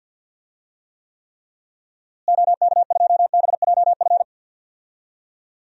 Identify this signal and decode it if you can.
Morse code, heard as a keyed tone, about 49 wpm, 700 Hz. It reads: OK2BJF
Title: DXCC-100 Call Signs - 6 Characters - Repeated 3x - 50wpm